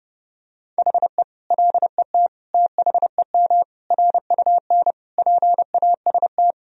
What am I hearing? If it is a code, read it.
Morse code, heard as a keyed tone, about 30 wpm, 700 Hz. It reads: HE LET THEM RUN PAST
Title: Sentences from Top 1000 Words - Review Part 3 - 30wpm